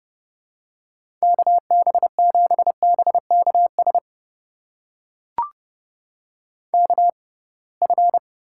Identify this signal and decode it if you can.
Morse code, heard as a keyed tone, about 30 wpm, spaced as for 23 wpm, 700 Hz. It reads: KB7BKS KF
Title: USA Call Signs - 6 Characters - 6x Character Spacing - 30wpm